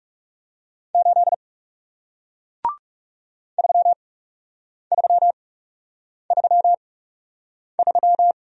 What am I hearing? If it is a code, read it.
Morse code, heard as a keyed tone, about 40 wpm, spaced as for 8 wpm, 700 Hz. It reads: 8 3333